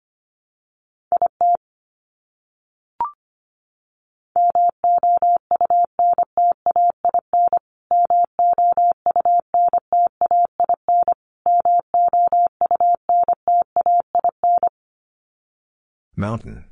Morse code, heard as a keyed tone, about 25 wpm, 700 Hz. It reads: IT MOUNTAIN MOUNTAIN MOUNTAIN
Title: Top 500 Words Repeated 25wpm